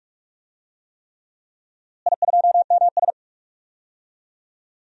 Morse code, heard as a keyed tone, about 46 wpm, 700 Hz. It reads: I2MS
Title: DXCC-100 Call Signs - 4 Characters - Repeated 3x - 45wpm